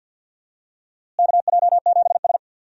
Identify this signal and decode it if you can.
Morse code, heard as a keyed tone, about 47 wpm, 700 Hz. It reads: KJ7S
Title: USA Call Signs - 4 Characters - Repeated 2x - 50wpm